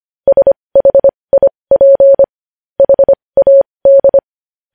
Morse code, heard as a keyed tone, about 25 wpm, spaced as for 20 wpm, 550 Hz. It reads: SHIP HAD
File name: podcast / News headlines-1x-25x20